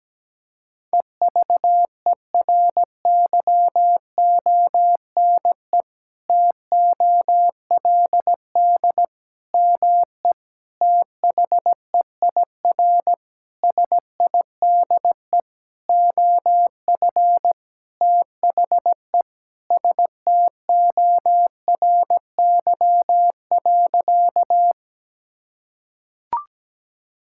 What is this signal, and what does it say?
Morse code, heard as a keyed tone, about 17 wpm, 700 Hz. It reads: EVERYONE TOLD ME THEIR SIDE OF THE STORY.
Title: Sentences from Top 1000 Words - Review Part 5 - 17wpm